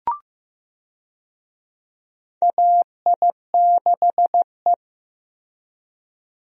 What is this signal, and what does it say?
Morse code, heard as a keyed tone, about 15 wpm, 700 Hz. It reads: AI6E
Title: USA Call Signs - 4 Characters - Repeated 3x - 15wpm